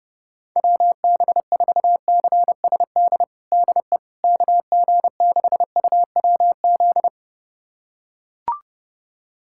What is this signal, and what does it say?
Morse code, heard as a keyed tone, about 32 wpm, 700 Hz. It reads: WB4CSD DE KG6UWZ
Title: USA Callsign DE Callsign 30wpm